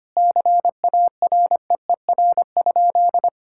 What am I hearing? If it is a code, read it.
Morse code, heard as a keyed tone, about 25 wpm, 700 Hz. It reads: CAREER?
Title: Sentences from Top 1000 Words - Review Part 6 - 25wpm